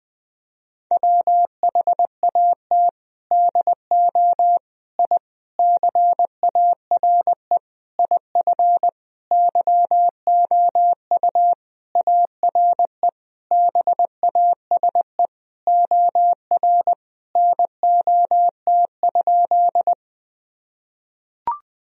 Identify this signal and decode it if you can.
Morse code, heard as a keyed tone, about 20 wpm, 700 Hz. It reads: WHAT DO I CARE IF YOU ARE BASE OR NOT?
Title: Sentences from Top 1000 Words - Review Part 1 - 20wpm